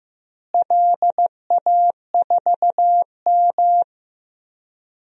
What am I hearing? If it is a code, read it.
Morse code, heard as a keyed tone, about 16 wpm, 700 Hz. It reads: LA4M
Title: DXCC-100 Call Signs - 4 Characters - Repeated 3x - 15wpm